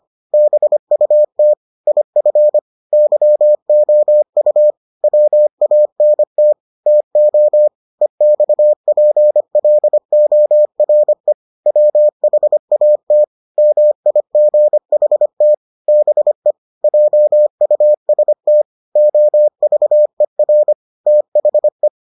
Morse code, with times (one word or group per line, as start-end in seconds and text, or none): 0.3-1.5 BUT
1.9-2.6 IF
2.9-4.7 YOU
5.0-6.5 WANT
6.9-7.7 TO
8.0-11.3 EXPLORE
11.7-13.2 WHAT
13.6-15.5 MIGHT
15.9-16.5 BE
16.8-18.6 JUST
19.0-20.7 OVER
21.1-21.9 THE